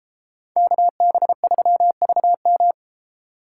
Morse code, heard as a keyed tone, about 33 wpm, 700 Hz. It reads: KB3VM